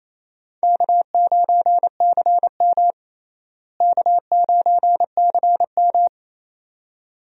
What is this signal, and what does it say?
Morse code, heard as a keyed tone, about 28 wpm, 700 Hz. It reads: K9CM K9CM
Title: USA Call Signs - 4 Characters - Repeated 2x - 28wpm